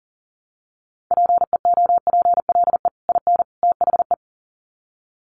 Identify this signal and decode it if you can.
Morse code, heard as a keyed tone, about 40 wpm, 700 Hz. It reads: PEOPLE IN THE
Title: Sets of 3 Words 40wpm